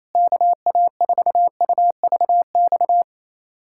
Morse code, heard as a keyed tone, about 28 wpm, 700 Hz. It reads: KA4UVX
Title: USA Callsign DE Callsign - 6 Characters - 28wpm